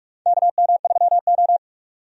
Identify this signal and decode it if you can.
Morse code, heard as a keyed tone, about 42 wpm, 700 Hz. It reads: KM3O
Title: USA Call Signs - 4 Characters - 45wpm